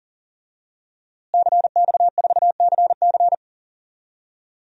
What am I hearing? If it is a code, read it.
Morse code, heard as a keyed tone, about 40 wpm, 700 Hz. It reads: CX4CC
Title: DXCC-100 Call Signs - 5 Characters - Repeated 2x - 40wpm